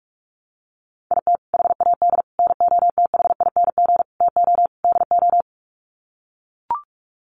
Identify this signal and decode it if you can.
Morse code, heard as a keyed tone, about 44 wpm, 700 Hz. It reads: IT HAD NOTHING TO DO
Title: Sets of 5 Words Encore2 45wpm